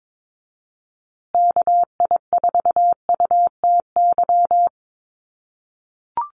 Morse code, heard as a keyed tone, about 22 wpm, 700 Hz. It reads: KI4UTY
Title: USA Call Signs Repeated 2x 22wpm